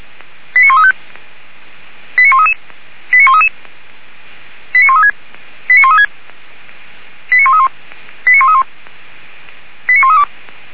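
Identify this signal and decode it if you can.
Five-tone selective call (ZVEI), heard as quick runs of five tones, tones of about 80 ms. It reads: 89135 89133 89133 89136 89136 89131 89131 89132
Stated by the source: ZVEI five-tone selective call 89134 received by SDR.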